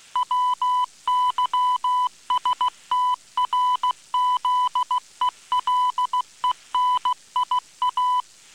Morse code, heard as a keyed tone, about 16 wpm, 1000 Hz. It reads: WYSTRZELENIA